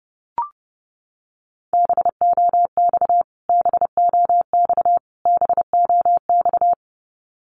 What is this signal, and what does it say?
Morse code, heard as a keyed tone, about 30 wpm, 700 Hz. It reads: BOX BOX BOX
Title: Top 500 Words Repeated 30wpm